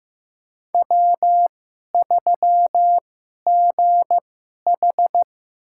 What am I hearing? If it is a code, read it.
Morse code, heard as a keyed tone, about 15 wpm, spaced as for 11 wpm, 700 Hz. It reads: W3GH
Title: USA Call Signs - 2x Character Spacing - 15wpm